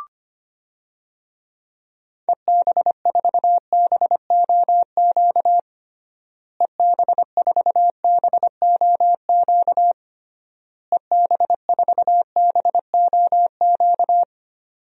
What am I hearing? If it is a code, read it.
Morse code, heard as a keyed tone, about 25 wpm, 700 Hz. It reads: EB4BOQ EB4BOQ EB4BOQ
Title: DXCC-100 Call Signs - 6 Characters - Repeated 3x - 25wpm